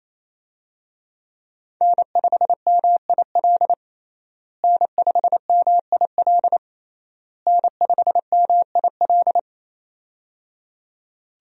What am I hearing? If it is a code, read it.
Morse code, heard as a keyed tone, about 28 wpm, 700 Hz. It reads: N5MIL N5MIL N5MIL
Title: USA Call Signs - 5 Characters - Repeated 3x - 28wpm